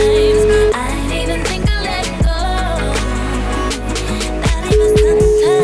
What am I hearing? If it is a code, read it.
Ringback tone (Europe, 425 Hz), with noise.